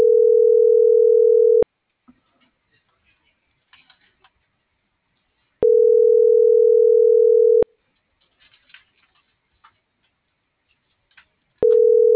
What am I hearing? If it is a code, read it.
Ringback tone (North America), 440 + 480 Hz.